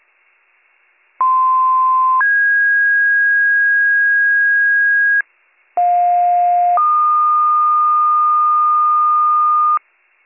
Two-tone page, heard as a one-second tone, then a three-second tone, about 1040 then 1650 Hz, about 710 then 1220 Hz. It is heard from 1.2 to 9.8 s.